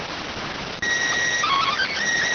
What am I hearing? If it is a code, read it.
SSTV (slow-scan television), PD 290.